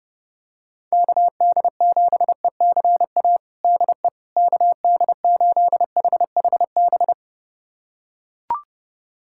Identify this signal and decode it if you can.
Morse code, heard as a keyed tone, about 30 wpm, 700 Hz. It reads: KD7ECA DE KD8HHB
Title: USA Callsign DE Callsign - 6 Characters - 30wpm